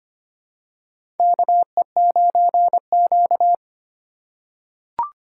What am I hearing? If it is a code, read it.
Morse code, heard as a keyed tone, about 26 wpm, 700 Hz. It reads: KE9Q
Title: USA Call Signs - 4 Characters - 2x Character Spacing - 25wpm